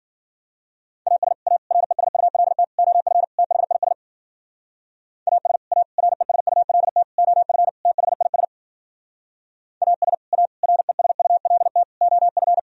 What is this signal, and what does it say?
Morse code, heard as a keyed tone, about 52 wpm, 700 Hz. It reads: AS A RESULT OF THIS AS A RESULT OF THIS AS A RESULT OF